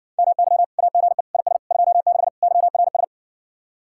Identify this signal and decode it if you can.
Morse code, heard as a keyed tone, about 60 wpm, 700 Hz. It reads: MY AGE IS 26 YRS